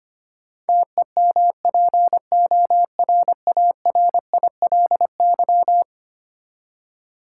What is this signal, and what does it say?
Morse code, heard as a keyed tone, about 25 wpm, 700 Hz. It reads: TEMPORARILY
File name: practice / 25wpm / Eleven Letter Words - 25wpm